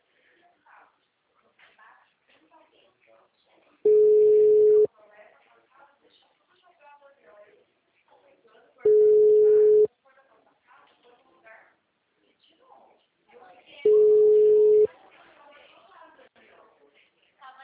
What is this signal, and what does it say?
Ringback tone (Europe, 425 Hz), with speech.